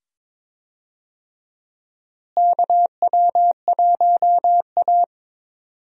Morse code, heard as a keyed tone, about 22 wpm, 700 Hz. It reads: KW1A
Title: USA Call Signs - 4 Characters - Repeated 2x - 22wpm